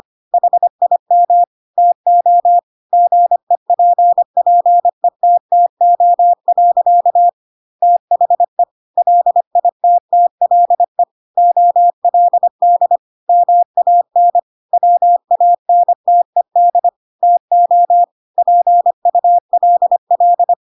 Morse code, heard as a keyed tone, about 25 wpm, 700 Hz. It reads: HIM TO GEPPETTO. THE LITTLE OLD MAN WANTED TO PULL
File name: Pinocchio / Pinocchio ch3 25 25 ik1hge